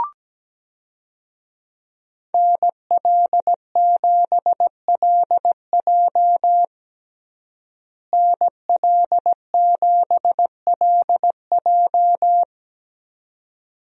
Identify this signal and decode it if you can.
Morse code, heard as a keyed tone, about 17 wpm, 700 Hz. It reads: NL7LJ NL7LJ
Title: USA Call Signs - 5 Characters - Repeated 2x - 17wpm